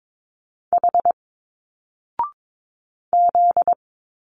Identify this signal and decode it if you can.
Morse code, heard as a keyed tone, about 22 wpm, 700 Hz. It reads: H Z
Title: Single Letters 22wpm